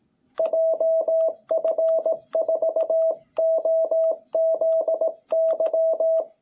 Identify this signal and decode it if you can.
Morse code, heard as a keyed tone, about 18 wpm, 660 Hz. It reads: JF4OZY
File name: jh8cbh / 2021cw